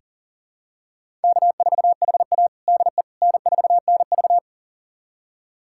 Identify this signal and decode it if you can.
Morse code, heard as a keyed tone, about 40 wpm, 700 Hz. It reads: K4HA DE N4NV
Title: USA Callsign DE Callsign - 4 Characters - 40wpm